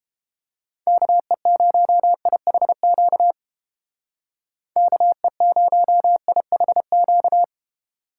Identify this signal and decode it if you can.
Morse code, heard as a keyed tone, about 30 wpm, 700 Hz. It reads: KE0IHQ KE0IHQ